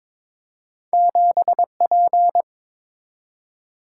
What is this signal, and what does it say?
Morse code, heard as a keyed tone, about 22 wpm, 690 Hz. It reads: 7P
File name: practice / 22wpm / Two Letter-Number Combinations - 22wpm